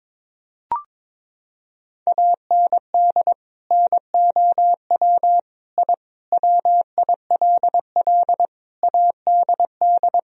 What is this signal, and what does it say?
Morse code, heard as a keyed tone, about 22 wpm, 700 Hz. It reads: AND NOW I WILL ADD
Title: Sentences from Top 1000 Words - Review Part 1 - 22wpm